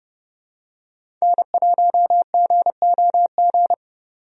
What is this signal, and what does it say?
Morse code, heard as a keyed tone, about 30 wpm, 700 Hz. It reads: N1GOG
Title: USA Call Signs - 5 Characters - 30wpm